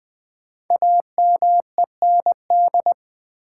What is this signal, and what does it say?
Morse code, heard as a keyed tone, about 20 wpm, 700 Hz. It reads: AMEND
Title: 5 Letter Words - First 3 Letters Given - 20wpm